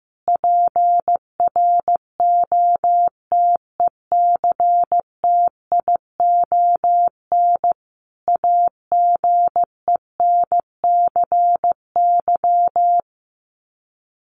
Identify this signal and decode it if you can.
Morse code, heard as a keyed tone, about 15 wpm, 700 Hz. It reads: PROTECTION AGENCY
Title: Sets of 5 Words Encore2 15wpm